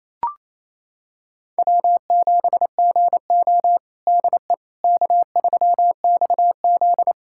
Morse code, heard as a keyed tone, about 28 wpm, 700 Hz. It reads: W7GO DE K3XZ
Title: USA Callsign DE Callsign - 4 Characters - 28wpm